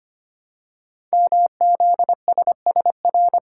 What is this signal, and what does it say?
Morse code, heard as a keyed tone, about 25 wpm, 700 Hz. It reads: MZSSR
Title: Code Groups - 5 Letters - 25wpm